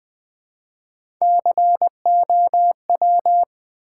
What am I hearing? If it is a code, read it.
Morse code, heard as a keyed tone, about 20 wpm, 700 Hz. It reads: COW
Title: Easy Words with Hints - 20wpm